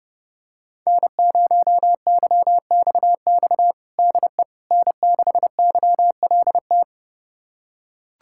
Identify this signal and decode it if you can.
Morse code, heard as a keyed tone, about 30 wpm, 700 Hz. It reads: N0YXX DE N6YLT
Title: USA Callsign DE Callsign - 5 Characters - 30wpm